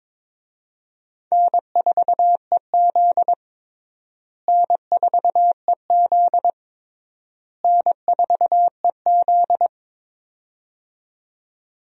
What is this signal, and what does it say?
Morse code, heard as a keyed tone, about 22 wpm, 700 Hz. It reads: N4EZ N4EZ N4EZ
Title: USA Call Signs - 4 Characters - Repeated 3x - 22wpm